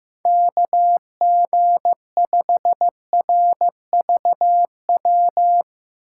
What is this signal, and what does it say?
Morse code, heard as a keyed tone, about 15 wpm, 700 Hz. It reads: KG5RVW
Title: USA Call Signs - 6 Characters - 3x Character Spacing - 15wpm